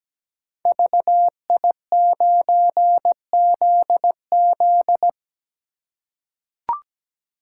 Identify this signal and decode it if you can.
Morse code, heard as a keyed tone, about 17 wpm, 700 Hz. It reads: VI9ZZ
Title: Code Groups - 5 Letter:Numbers - 17wpm